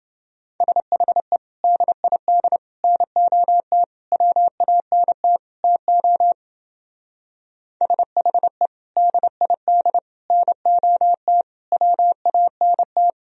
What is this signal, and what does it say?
Morse code, heard as a keyed tone, about 28 wpm, 700 Hz. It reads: SHE DID NOT WANT TO SHE DID NOT WANT